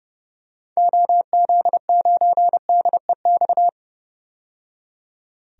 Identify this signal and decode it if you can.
Morse code, heard as a keyed tone, about 32 wpm, 700 Hz. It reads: OZ9DEX
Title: DXCC-100 Call Signs - 6 Characters - 30wpm